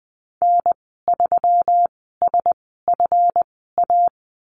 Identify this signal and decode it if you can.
Morse code, heard as a keyed tone, about 20 wpm, spaced as for 15 wpm, 700 Hz. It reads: N3SFA
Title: 245 Lesson 39g - Letter z - Call Signs 20wpm